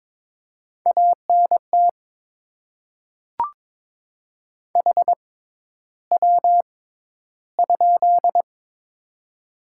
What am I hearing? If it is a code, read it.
Morse code, heard as a keyed tone, about 22 wpm, spaced as for 11 wpm, 700 Hz. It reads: ANT HW?
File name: Top 20 QSO Elements - 6x Character Spacing - 22wpm